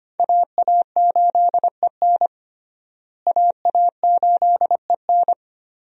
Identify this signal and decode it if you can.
Morse code, heard as a keyed tone, about 25 wpm, 700 Hz. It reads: AA8EN AA8EN
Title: USA Call Signs - 5 Characters - Repeated 2x - 25wpm